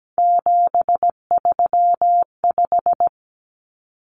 Morse code, heard as a keyed tone, about 17 wpm, 700 Hz. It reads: 735
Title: Common Cut Numbers - 3-Digit 17wpm